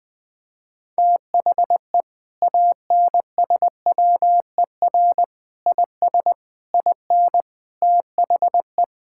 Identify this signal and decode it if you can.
Morse code, heard as a keyed tone, about 20 wpm, 700 Hz. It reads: THE ANSWER IS IN THE
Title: Sentences from Top 1000 Words - Review Part 5 - 20wpm